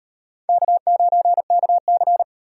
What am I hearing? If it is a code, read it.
Morse code, heard as a keyed tone, about 38 wpm, 700 Hz. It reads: K9KC